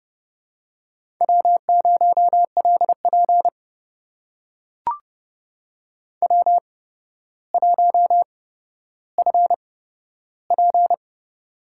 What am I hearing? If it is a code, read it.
Morse code, heard as a keyed tone, about 30 wpm, spaced as for 13 wpm, 700 Hz. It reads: W0LP W1FP
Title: USA Call Signs - 4 Characters - 8x Character Spacing - 30wpm